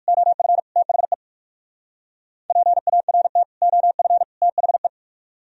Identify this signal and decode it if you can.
Morse code, heard as a keyed tone, about 57 wpm, 700 Hz. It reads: OF THE PART OF THE